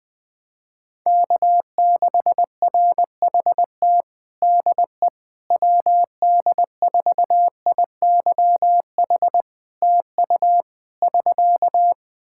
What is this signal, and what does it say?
Morse code, heard as a keyed tone, about 20 wpm, 700 Hz. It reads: K6RHT DE WD4IYH TU <SK>
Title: Practice QSO - Exchange 4 - Ending - 20wpm